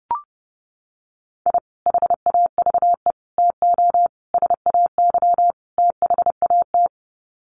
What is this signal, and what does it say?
Morse code, heard as a keyed tone, about 30 wpm, 700 Hz. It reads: I HAVE TO SAY THAT
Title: Sets of 5 Words Encore 30wpm